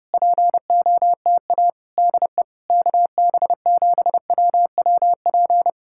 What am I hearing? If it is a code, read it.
Morse code, heard as a keyed tone, about 28 wpm, 700 Hz. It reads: POTA DE KB7WWP
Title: Speed-Racing Common QSO Phrases - 30wpm